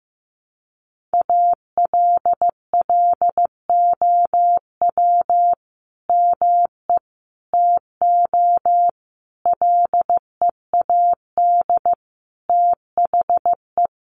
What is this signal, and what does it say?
Morse code, heard as a keyed tone, about 15 wpm, 700 Hz. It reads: ALLOW ME TO LEAD THE
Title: Sentences from Top 700 Words 15wpm